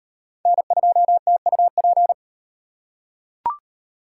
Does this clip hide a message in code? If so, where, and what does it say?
Yes: Morse code from 0.5 to 2.1 s, heard as a keyed tone, about 38 wpm, 700 Hz. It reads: N2TUP